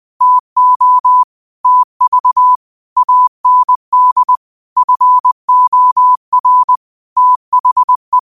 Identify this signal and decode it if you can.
Morse code, heard as a keyed tone, about 20 wpm, 1000 Hz. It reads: TO TV AND FOR THE
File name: qotdfiles / 20wpm